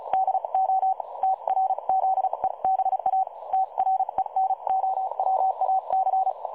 Morse code, heard as a keyed tone, about 36 wpm, spaced as for 31 wpm, 760 Hz. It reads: DO TRZEBA TRENOWAC
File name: dlazaawansowanych / nagraniazpasma / websdr recording 3565.0kHz